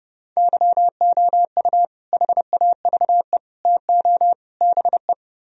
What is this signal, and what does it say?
Morse code, heard as a keyed tone, about 32 wpm, 700 Hz. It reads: YOU HAVE TO BE